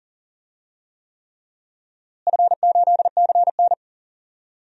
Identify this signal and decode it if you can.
Morse code, heard as a keyed tone, about 40 wpm, 700 Hz. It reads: F8CN
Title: DXCC-100 Call Signs - 4 Characters - Repeated 3x - 40wpm